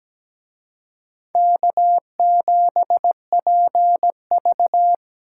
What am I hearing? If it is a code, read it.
Morse code, heard as a keyed tone, about 20 wpm, 700 Hz. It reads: K7PV
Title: Speed Racing - USA Call Signs 17wpm